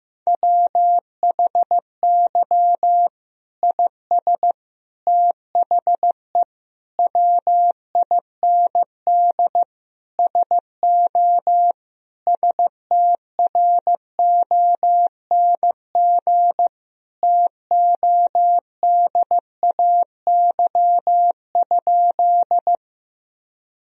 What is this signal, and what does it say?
Morse code, heard as a keyed tone, about 15 wpm, 700 Hz. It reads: WHY IS THE WIND SO STRONG TODAY?
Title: Sentences from Top 1000 Words - Review Part 2 - 15wpm